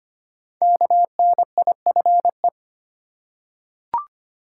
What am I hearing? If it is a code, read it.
Morse code, heard as a keyed tone, about 25 wpm, 700 Hz. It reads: KNIFE
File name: 5 Letter Words - First 3 Letters Given - 25wpm